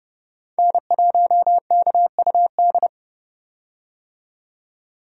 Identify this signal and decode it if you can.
Morse code, heard as a keyed tone, about 30 wpm, 700 Hz. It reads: N1KUD
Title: USA Call Signs - 5 Characters - 30wpm